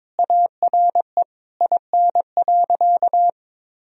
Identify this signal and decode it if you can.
Morse code, heard as a keyed tone, about 22 wpm, 700 Hz. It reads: ARE IN.
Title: Sentences from Top 1000 Words - Review Part 2 - 22wpm